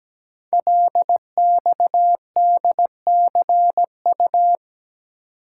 Morse code, heard as a keyed tone, about 17 wpm, 700 Hz. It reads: LXDCU
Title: Code Groups - 5 Letters - 17wpm